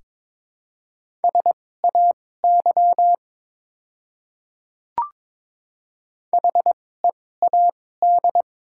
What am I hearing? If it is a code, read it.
Morse code, heard as a keyed tone, about 22 wpm, spaced as for 16 wpm, 700 Hz. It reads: SAY HEAD
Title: 2x Character Spacing - Top 300 Words - 22wpm